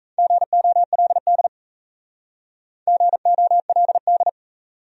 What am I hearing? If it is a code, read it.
Morse code, heard as a keyed tone, about 35 wpm, 700 Hz. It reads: GOLD GOLD